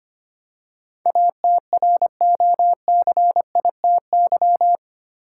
Morse code, heard as a keyed tone, about 25 wpm, 700 Hz. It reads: ATROCITY
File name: 8 Letter Words - First 3 Letters Given - 25wpm